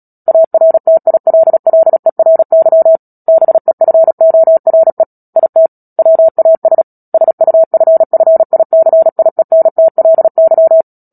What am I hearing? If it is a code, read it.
Morse code, heard as a keyed tone, about 37 wpm, 650 Hz. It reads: ARTILLERY BEFORE IT WAS SUFFICIENTLY
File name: Book Two / 10-Chapter 10 - Epilogue - 36 WPM